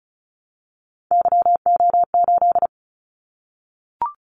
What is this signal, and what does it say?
Morse code, heard as a keyed tone, about 36 wpm, 700 Hz. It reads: YO8
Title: Callsign Prefixes for Top 25 DXCC - 35wpm